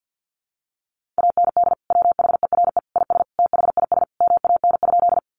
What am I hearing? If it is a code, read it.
Morse code, heard as a keyed tone, about 50 wpm, 700 Hz. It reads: AND WHERE IS THIS MAN?
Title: Sentences from Top 200 Words 50wpm